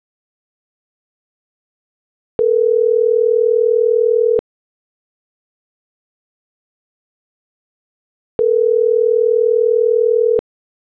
Ringback tone (North America), 440 + 480 Hz.